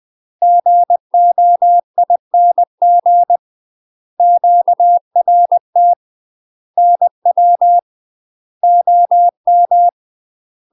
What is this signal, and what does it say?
Morse code, heard as a keyed tone, about 20 wpm, 700 Hz. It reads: GOING QRT NW OM
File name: UK MASTER QSOs TRAINING / 20WPM / UK MASTER QSOs 0024